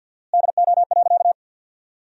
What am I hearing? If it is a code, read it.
Morse code, heard as a keyed tone, about 50 wpm, 700 Hz. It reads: DO.
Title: 4x Word-Spacing - Sentences from Top 100 Words - 50wpm